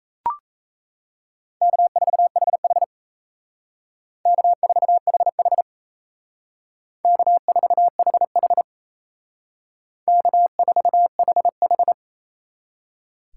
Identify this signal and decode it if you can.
Morse code, heard as a keyed tone, about 35 wpm, 700 Hz. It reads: K4HH K4HH K4HH K4HH